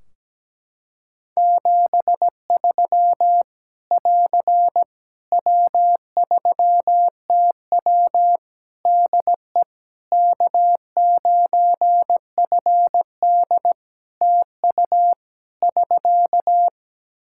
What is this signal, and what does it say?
Morse code, heard as a keyed tone, about 17 wpm, 700 Hz. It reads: 73 <AR> W3TW DE K9FD TU <SK>
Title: Practice QSO - Exchange 4 - Ending - 17wpm